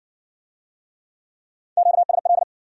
Morse code, heard as a keyed tone, about 56 wpm, 700 Hz. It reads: QSL